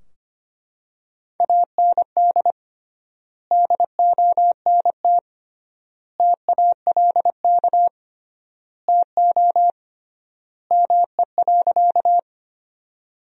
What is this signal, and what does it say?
Morse code, heard as a keyed tone, about 25 wpm, 700 Hz. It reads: AND DONT TALK TO ME.
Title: 3x Word-Spacing - Sentences from Top 400 Words - 25wpm